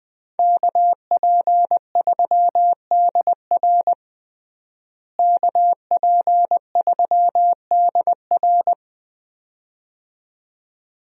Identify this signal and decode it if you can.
Morse code, heard as a keyed tone, about 21 wpm, 700 Hz. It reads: KP3DR KP3DR
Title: USA Call Signs - 5 Characters - Repeated 2x - 20wpm